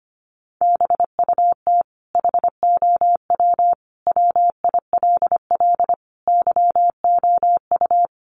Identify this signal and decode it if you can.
Morse code, heard as a keyed tone, about 25 wpm, 700 Hz. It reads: BUT HOW WILL YOU
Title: Sentences from Top 1100 Words 25wpm